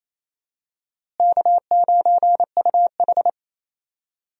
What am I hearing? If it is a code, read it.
Morse code, heard as a keyed tone, about 28 wpm, 700 Hz. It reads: K9UH
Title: USA Call Signs - 4 Characters - Repeated 2x - 28wpm